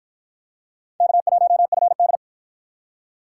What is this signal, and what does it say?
Morse code, heard as a keyed tone, about 53 wpm, 700 Hz. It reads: K1FD